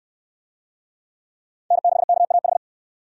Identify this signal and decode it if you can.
Morse code, heard as a keyed tone, about 68 wpm, 700 Hz. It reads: N6KAH